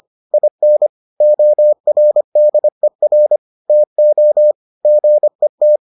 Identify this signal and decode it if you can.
Morse code, heard as a keyed tone, about 25 wpm, 600 Hz. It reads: IN ORDER TO GET